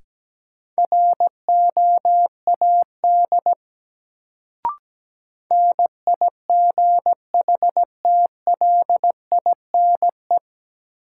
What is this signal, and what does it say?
Morse code, heard as a keyed tone, about 17 wpm, 700 Hz. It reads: ROAD NIGHTLINE
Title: Sets of 5 Words - New Word Drill - 17wpm